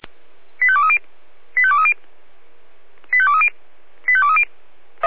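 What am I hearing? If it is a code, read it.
Five-tone selective call (ZVEI), heard as quick runs of five tones, tones of about 70 ms. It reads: 85230 85230 85230 85230
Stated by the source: SDR-received ZVEI five-tone selective call, code 85820.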